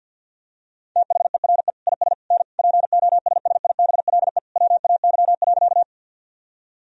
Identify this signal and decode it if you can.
Morse code, heard as a keyed tone, about 52 wpm, 700 Hz. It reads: THERE IS N POSSIBLE WAY.
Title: Sentences from Top 500 Words and Abbreviations 50wpm